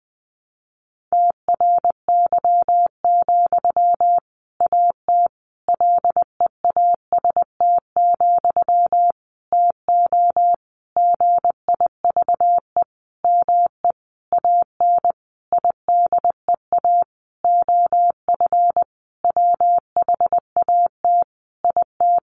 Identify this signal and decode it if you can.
Morse code, heard as a keyed tone, about 20 wpm, 700 Hz. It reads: TRY, AT LEAST, TO GIVE ME AN IDEA OF WHAT IT
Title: Sentences from Top 700 Words 20wpm